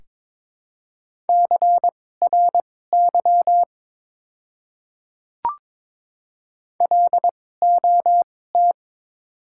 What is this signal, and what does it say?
Morse code, heard as a keyed tone, about 22 wpm, spaced as for 16 wpm, 700 Hz. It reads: CRY LOT
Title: 2x Character Spacing - Top 500 Words - 22wpm